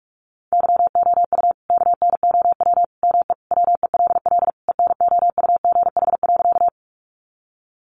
Morse code, heard as a keyed tone, about 44 wpm, 700 Hz. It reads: YOU KNOW ME WELL ENOUGH.
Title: Sentences from Top 400 Words 45wpm